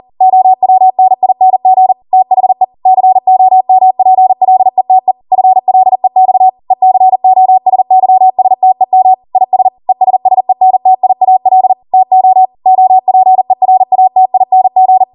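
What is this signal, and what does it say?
Morse code, heard as a keyed tone, about 40 wpm, 750 Hz. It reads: OWNING THE COMPLETE FLEX ECOSYSTEM IS ESSENTIAL TO OPERATING
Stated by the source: Code practice at 40 wpm.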